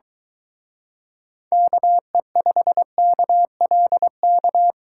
Morse code, heard as a keyed tone, about 23 wpm, 700 Hz. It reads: KE5KLK